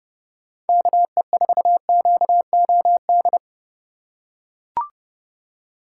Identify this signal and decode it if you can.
Morse code, heard as a keyed tone, about 30 wpm, 700 Hz. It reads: KE4QOD
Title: USA Call Signs - 6 Characters - 3x Character Spacing - 30wpm